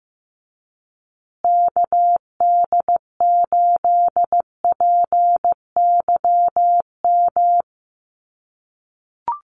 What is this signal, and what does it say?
Morse code, heard as a keyed tone, about 15 wpm, 700 Hz. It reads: KD8PYM
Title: USA Call Signs Repeated 2x 15wpm